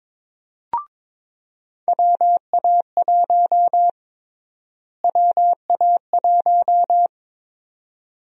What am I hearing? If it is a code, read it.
Morse code, heard as a keyed tone, about 22 wpm, 700 Hz. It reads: WA1 WA1
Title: USA Call Sign Prefixes - Repeated 2x - 22wpm